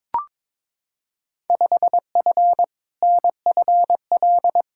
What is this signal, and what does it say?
Morse code, heard as a keyed tone, about 22 wpm, 700 Hz. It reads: 5F NFL
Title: Field Day Exchanges Encore - 22wpm